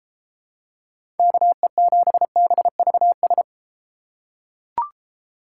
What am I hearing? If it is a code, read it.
Morse code, heard as a keyed tone, about 32 wpm, 700 Hz. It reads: KE7BVS